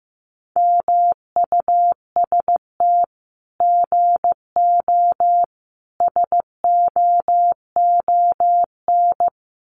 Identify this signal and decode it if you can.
Morse code, heard as a keyed tone, about 15 wpm, 700 Hz. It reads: MUST GO SOON
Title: Common QSO Phrases 15wpm